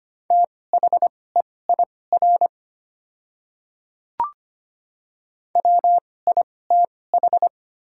Morse code, heard as a keyed tone, about 25 wpm, spaced as for 18 wpm, 700 Hz. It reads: THEIR WITH
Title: 2x Character Spacing - Top 100 Words - 25wpm